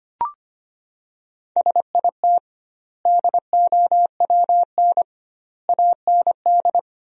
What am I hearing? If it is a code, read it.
Morse code, heard as a keyed tone, about 25 wpm, 700 Hz. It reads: SIT DOWN AND
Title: 2x Word-Spacing - Sentences from Top 500 Words - 25wpm